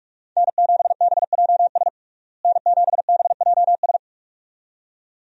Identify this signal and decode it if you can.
Morse code, heard as a keyed tone, about 45 wpm, 700 Hz. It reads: N7BJS N7BJS
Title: USA Call Signs - 5 Characters - Repeated 2x - 45wpm